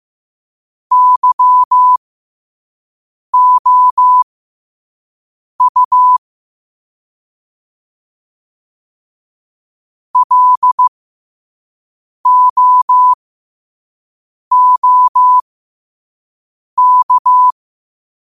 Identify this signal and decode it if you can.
Morse code, heard as a keyed tone, about 15 wpm, spaced as for 5 wpm, 1000 Hz. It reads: YOU LOOK
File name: qotdfiles / 5wpm